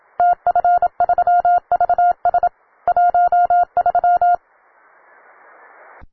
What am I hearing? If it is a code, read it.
Morse code, heard as a keyed tone, about 27 wpm, 710 Hz. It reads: TF3VS 13
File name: wae / cw beginner / ua9cdv cw